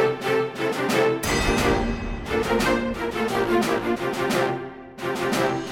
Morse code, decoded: none